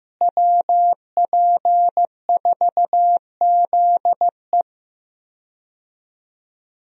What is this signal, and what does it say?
Morse code, heard as a keyed tone, about 15 wpm, 700 Hz. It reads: WP4ZE
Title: USA Call Signs - 5 Characters - Repeated 3x - 15wpm